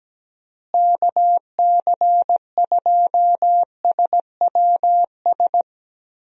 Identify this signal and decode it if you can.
Morse code, heard as a keyed tone, about 17 wpm, 700 Hz. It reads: KC2SWS
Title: USA Call Signs - 6 Characters - Repeated 3x - 17wpm